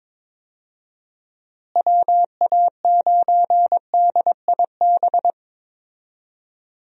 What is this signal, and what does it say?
Morse code, heard as a keyed tone, about 22 wpm, 700 Hz. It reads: WA9DIB
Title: USA Call Signs - 6 Characters - Repeated 2x - 22wpm